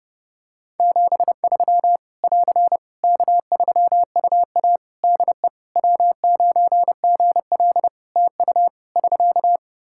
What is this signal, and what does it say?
Morse code, heard as a keyed tone, about 28 wpm, 700 Hz. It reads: 73 <AR> K3UA DE W9GL TU <SK>
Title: Practice QSO - Exchange 4 - Ending - 30wpm